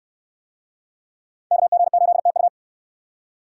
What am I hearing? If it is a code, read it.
Morse code, heard as a keyed tone, about 68 wpm, 700 Hz. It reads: KG8TU